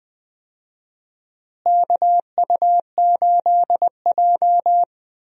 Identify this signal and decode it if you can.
Morse code, heard as a keyed tone, about 20 wpm, 700 Hz. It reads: KU8J
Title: USA Call Signs - 4 Characters - Repeated 2x - 20wpm